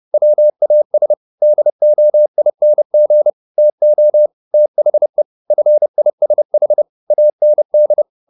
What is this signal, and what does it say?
Morse code, heard as a keyed tone, about 30 wpm, 600 Hz. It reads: WAS DOING TO THE FISH AND